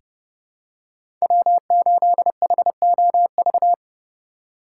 Morse code, heard as a keyed tone, about 30 wpm, 700 Hz. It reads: W8HOV
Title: USA Call Signs - 5 Characters - 3x Character Spacing - 30wpm